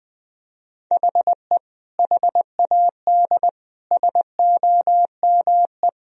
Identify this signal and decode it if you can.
Morse code, heard as a keyed tone, about 20 wpm, 700 Hz. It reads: HE HAD SOME
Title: Sentences from Top 1000 Words - Review Part 3 - 20wpm